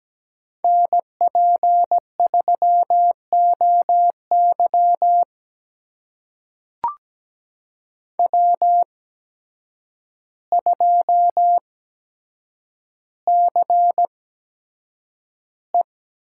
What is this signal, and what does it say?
Morse code, heard as a keyed tone, about 17 wpm, spaced as for 8 wpm, 700 Hz. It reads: NP3OY W2CE
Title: USA Call Signs - 8x Character Spacing - 17wpm